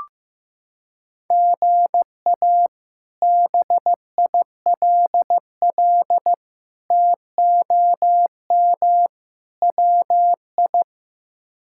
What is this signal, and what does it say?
Morse code, heard as a keyed tone, about 15 wpm, 700 Hz. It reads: GA BILL TOM WI
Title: K1USN SST QSO Practice - 15wpm